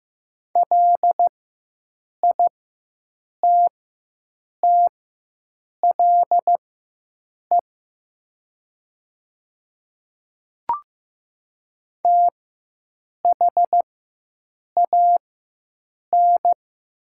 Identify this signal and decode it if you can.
Morse code, heard as a keyed tone, about 15 wpm, spaced as for 7 wpm, 700 Hz. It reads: LITTLE THAN
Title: 4x Character Spacing - Top 300 Words - 15wpm